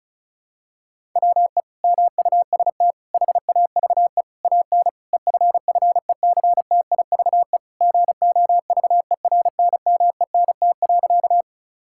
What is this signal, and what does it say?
Morse code, heard as a keyed tone, about 34 wpm, 700 Hz. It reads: WE MUST HAVE AN EFFECTIVE GOVERNMENT.
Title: Sentences from Top 1000 Words - Review Part 6 - 35wpm